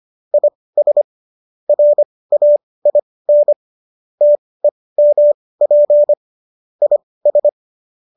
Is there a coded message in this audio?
Yes: Morse code from 0.3 to 7.5 s, heard as a keyed tone, about 25 wpm, spaced as for 18 wpm, 590 Hz. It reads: IS RAIN TEMP IS